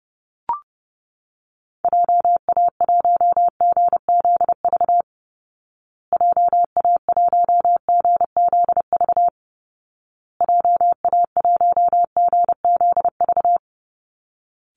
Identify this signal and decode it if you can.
Morse code, heard as a keyed tone, about 28 wpm, 700 Hz. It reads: JA1GZV JA1GZV JA1GZV
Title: Call Signs Repeated 2x - 30wpm